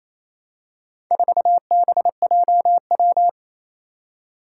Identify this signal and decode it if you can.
Morse code, heard as a keyed tone, about 28 wpm, 700 Hz. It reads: 4BJW
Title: Four Challenging Letter-Number Combinations 28wpm